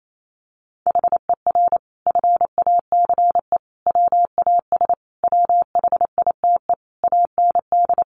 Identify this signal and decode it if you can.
Morse code, heard as a keyed tone, about 28 wpm, 700 Hz. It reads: HER FACE WAS WHITE AND
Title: Sentences from Top 1200 Words 28wpm